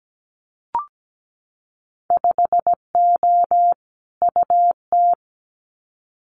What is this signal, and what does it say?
Morse code, heard as a keyed tone, about 17 wpm, 700 Hz. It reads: 5O UT
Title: Winter Field Day Exchanges 17wpm